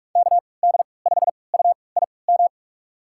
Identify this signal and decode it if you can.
Morse code, heard as a keyed tone, about 46 wpm, spaced as for 25 wpm, 700 Hz. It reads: KD5UIM
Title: USA Call Signs - 3x Character Spacing - 45wpm